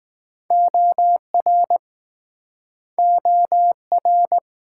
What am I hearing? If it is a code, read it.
Morse code, heard as a keyed tone, about 20 wpm, 700 Hz. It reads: OR OR